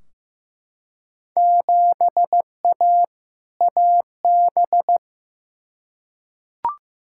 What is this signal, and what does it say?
Morse code, heard as a keyed tone, about 15 wpm, 700 Hz. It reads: 7A AB
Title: Field Day Exchanges Encore - 15wpm